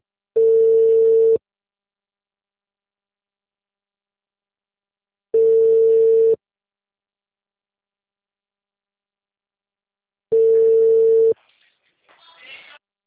Ringback tone (China, 450 Hz).